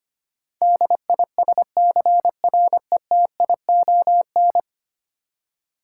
Morse code, heard as a keyed tone, about 25 wpm, 700 Hz. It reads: DISCRETION
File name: 10 Letter Words - First 3 Letters Given - 25wpm